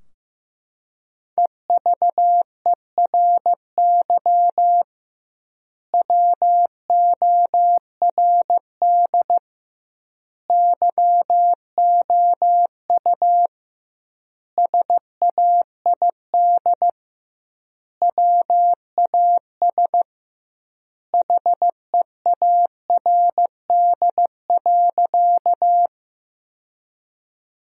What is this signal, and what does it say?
Morse code, heard as a keyed tone, about 16 wpm, 700 Hz. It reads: EVERY WORD YOU SAID WAS HEARD.
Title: 2x Word-Spacing - Sentences from Top 400 Words - 15wpm